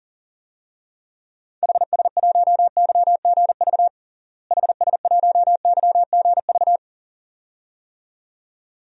Morse code, heard as a keyed tone, about 40 wpm, 700 Hz. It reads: HS1YGV HS1YGV
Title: DXCC-100 Call Signs - 6 Characters - Repeated 2x - 40wpm